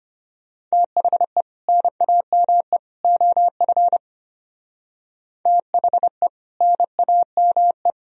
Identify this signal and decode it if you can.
Morse code, heard as a keyed tone, about 33 wpm, 700 Hz. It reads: THE NAME OF THE NAME